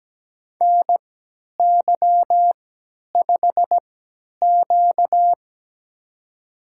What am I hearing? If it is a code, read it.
Morse code, heard as a keyed tone, about 17 wpm, spaced as for 10 wpm, 700 Hz. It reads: NY5Q
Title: USA Call Signs - 4 Characters - 3x Character Spacing - 17wpm